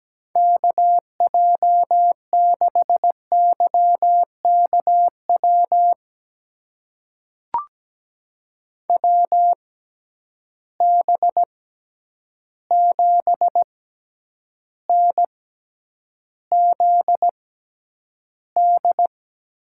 Morse code, heard as a keyed tone, about 17 wpm, spaced as for 9 wpm, 700 Hz. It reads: KJ6YKW WB7NZD
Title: USA Call Signs - 6 Characters - 6x Character Spacing - 17wpm